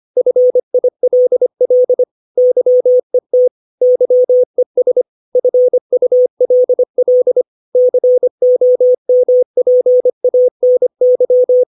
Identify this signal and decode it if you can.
Morse code, heard as a keyed tone, about 25 wpm, 500 Hz. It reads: FILL YET YES FULL COMPANY
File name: 632 words / 632 Words 25WPM